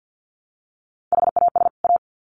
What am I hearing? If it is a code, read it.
Morse code, heard as a keyed tone, about 50 wpm, 700 Hz. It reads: HAS A
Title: Sets of 2 Words 50wpm